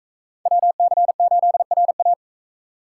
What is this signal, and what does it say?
Morse code, heard as a keyed tone, about 41 wpm, 700 Hz. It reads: WC8RA